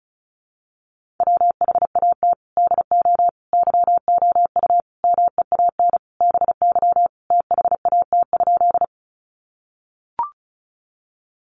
Morse code, heard as a keyed tone, about 34 wpm, 700 Hz. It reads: WHAT DO YOU MEAN BY THAT?
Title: Sentences from Top 200 Words 35wpm